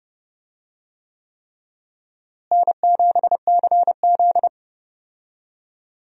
Morse code, heard as a keyed tone, about 30 wpm, 700 Hz. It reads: N7CZ
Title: USA Call Signs - 4 Characters - Repeated 3x - 30wpm